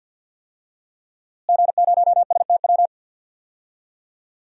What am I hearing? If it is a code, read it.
Morse code, heard as a keyed tone, about 46 wpm, 700 Hz. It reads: G0STW
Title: DXCC-100 Call Signs - 5 Characters - Repeated 3x - 50wpm